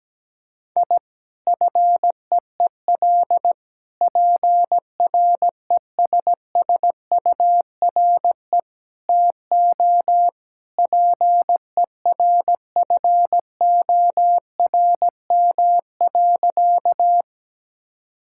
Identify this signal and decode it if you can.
Morse code, heard as a keyed tone, about 17 wpm, 700 Hz. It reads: I FEEL PRESSURE TO PERFORM.
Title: Sentences from Top 1000 Words - Review Part 6 - 17wpm